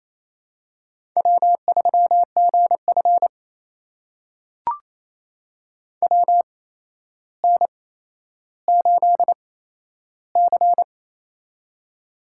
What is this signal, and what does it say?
Morse code, heard as a keyed tone, about 28 wpm, spaced as for 12 wpm, 700 Hz. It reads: W3GF WN8C
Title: USA Call Signs - 4 Characters - 8x Character Spacing - 28wpm